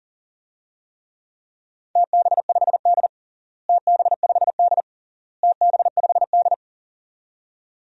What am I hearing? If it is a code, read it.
Morse code, heard as a keyed tone, about 40 wpm, 700 Hz. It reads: TB5D TB5D TB5D
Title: DXCC-100 Call Signs - 4 Characters - Repeated 3x - 40wpm